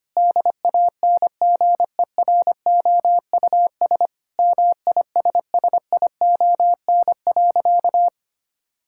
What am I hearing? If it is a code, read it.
Morse code, heard as a keyed tone, about 25 wpm, 700 Hz. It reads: DANGEROUS MISSION.
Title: Sentences from Top 1600 Words 25wpm